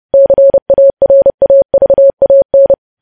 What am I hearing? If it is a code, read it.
Morse code, heard as a keyed tone, about 30 wpm, 550 Hz. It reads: CARAVAN
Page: news headlines in morse code at 30 wpm